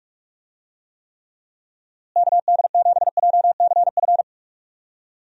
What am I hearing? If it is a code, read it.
Morse code, heard as a keyed tone, about 41 wpm, 700 Hz. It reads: KD7JCF
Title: USA Call Signs - 6 Characters - Repeated 2x - 45wpm